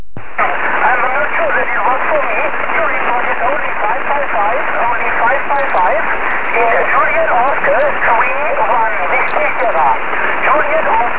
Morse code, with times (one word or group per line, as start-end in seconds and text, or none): none